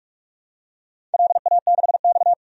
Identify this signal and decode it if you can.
Morse code, heard as a keyed tone, about 47 wpm, 700 Hz. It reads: LA6X
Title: DXCC-100 Call Signs - 4 Characters - 45wpm